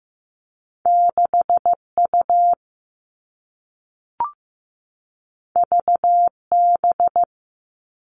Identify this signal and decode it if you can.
Morse code, heard as a keyed tone, about 15 wpm, 700 Hz. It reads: 6U VB
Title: All Cut Numbers - 2-Digit 15wpm